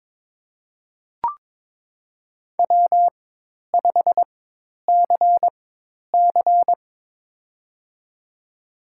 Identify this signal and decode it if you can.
Morse code, heard as a keyed tone, about 21 wpm, spaced as for 10 wpm, 700 Hz. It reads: W5CC
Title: USA Call Signs - 4 Characters - 4x Character Spacing - 22wpm